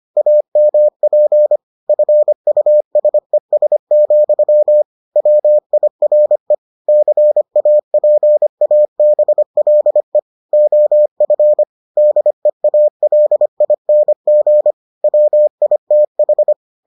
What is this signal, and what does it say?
Morse code, heard as a keyed tone, about 25 wpm, 600 Hz. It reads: AMP FUSES, WIRE CAPABLE OF DEALING WITH